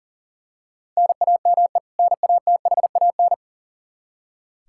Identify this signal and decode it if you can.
Morse code, heard as a keyed tone, about 40 wpm, 700 Hz. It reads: NAME NATHAN
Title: Practice QSO - Exchange 1 - Beginning - 40wpm